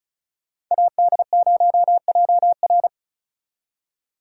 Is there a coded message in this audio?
Yes: Morse code from 0.7 to 2.9 s, heard as a keyed tone, about 44 wpm, 700 Hz. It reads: AD0JR